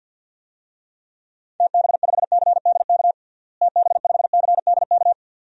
Morse code, heard as a keyed tone, about 52 wpm, 700 Hz. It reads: TB5CDK TB5CDK
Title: DXCC-100 Call Signs - 6 Characters - Repeated 2x - 50wpm